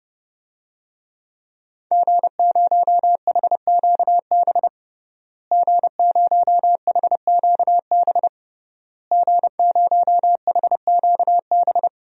Morse code, heard as a keyed tone, about 30 wpm, 700 Hz. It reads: G0HQB G0HQB G0HQB
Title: DXCC-100 Call Signs - 5 Characters - Repeated 3x - 30wpm